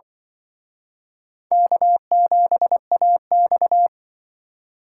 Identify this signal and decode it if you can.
Morse code, heard as a keyed tone, about 24 wpm, 700 Hz. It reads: K7AX
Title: Speed Racing - USA 1x2 Call Signs 20wpm